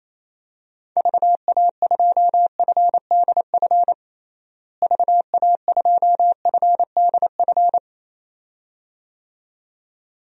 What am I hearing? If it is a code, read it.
Morse code, heard as a keyed tone, about 28 wpm, 700 Hz. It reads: VA2FDF VA2FDF
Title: DXCC-100 Call Signs - 6 Characters - Repeated 2x - 28wpm